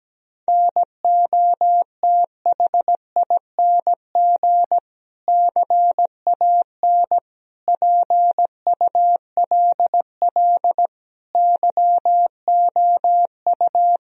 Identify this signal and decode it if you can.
Morse code, heard as a keyed tone, about 17 wpm, 700 Hz. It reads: NOTHING CAN PULL YOU
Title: Sentences from Top 1000 Words - Review Part 2 - 17wpm